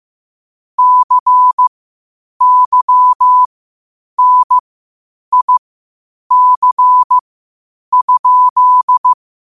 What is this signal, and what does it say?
Morse code, heard as a keyed tone, about 15 wpm, spaced as for 8 wpm, 1000 Hz. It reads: CYNIC?
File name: qotdfiles / 8wpm